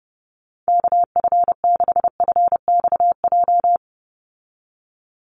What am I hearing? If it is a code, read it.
Morse code, heard as a keyed tone, about 30 wpm, 700 Hz. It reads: KF6FXJ
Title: USA Call Signs 30wpm